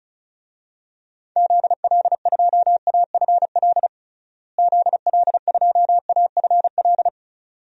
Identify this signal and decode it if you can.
Morse code, heard as a keyed tone, about 36 wpm, 700 Hz. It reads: ZL2AFL ZL2AFL
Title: DXCC-100 Call Signs - 6 Characters - Repeated 2x - 35wpm